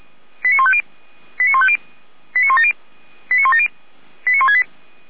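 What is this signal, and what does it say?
Five-tone selective call (ZVEI), heard as quick runs of five tones, tones of about 70 ms. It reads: 89150 89150 89170 89170 89168
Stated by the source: Five-tone ZVEI selective call received by SDR.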